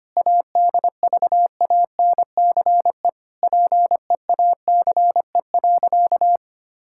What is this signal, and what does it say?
Morse code, heard as a keyed tone, about 25 wpm, 700 Hz. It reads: ADVANCE PEACE.
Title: Sentences from Top 1600 Words 25wpm